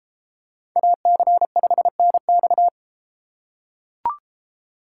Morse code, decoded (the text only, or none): AC5NX